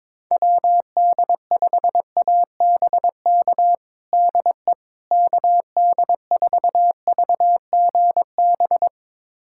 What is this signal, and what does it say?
Morse code, heard as a keyed tone, about 22 wpm, 700 Hz. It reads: WD5ABK DE KD4VGB
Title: USA Callsign DE Callsign 22wpm